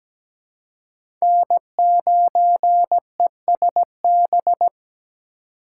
Morse code, heard as a keyed tone, about 17 wpm, 700 Hz. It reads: N9ESB
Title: USA Call Signs - 5 Characters - 2x Character Spacing - 17wpm